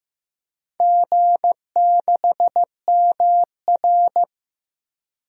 Morse code, heard as a keyed tone, about 15 wpm, 700 Hz. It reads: G6MR
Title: DXCC-100 Call Signs - 4 Characters - 15wpm